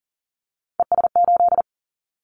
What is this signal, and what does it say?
Morse code, heard as a keyed tone, about 40 wpm, 700 Hz. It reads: ES8
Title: Callsign Prefixes for Top 100 DXCC - 40wpm